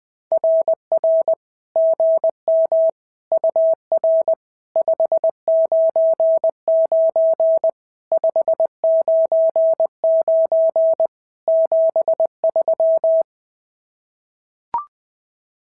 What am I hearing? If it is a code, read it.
Morse code, heard as a keyed tone, about 20 wpm, 650 Hz. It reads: RR GM UR 599 599 73
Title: SOTA QSO Practice 20wpm